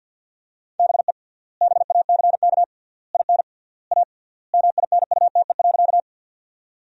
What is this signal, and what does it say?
Morse code, heard as a keyed tone, about 52 wpm, 700 Hz. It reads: BE BACK IN A MINUTE.
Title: 3x Word-Spacing - Sentences from Top 500 Words - 50wpm